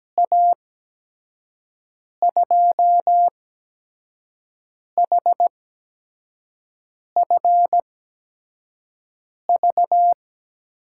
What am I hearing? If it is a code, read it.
Morse code, heard as a keyed tone, about 17 wpm, spaced as for 5 wpm, 700 Hz. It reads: A2HFV
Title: USA Call Signs - 8x Character Spacing - 17wpm